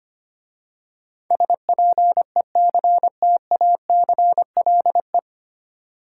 Morse code, heard as a keyed tone, about 25 wpm, 700 Hz. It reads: SPECTACLE
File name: practice / 25wpm / 9 Letter Words - First 3 Letters Given - 25wpm